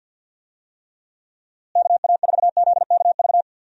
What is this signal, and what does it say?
Morse code, heard as a keyed tone, about 52 wpm, 700 Hz. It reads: KA4ZKV
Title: USA Call Signs - 6 Characters - Repeated 3x - 50wpm